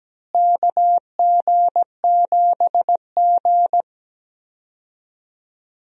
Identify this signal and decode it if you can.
Morse code, heard as a keyed tone, about 17 wpm, 700 Hz. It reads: KG7G
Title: USA Call Signs - 4 Characters - Repeated 2x - 17wpm